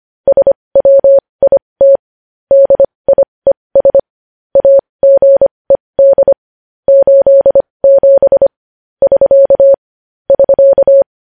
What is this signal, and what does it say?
Morse code, heard as a keyed tone, about 25 wpm, spaced as for 20 wpm, 550 Hz. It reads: SWIT DIES AGED 87 <SK> <SK>
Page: news-headlines-in-morse-code-at-20-wpm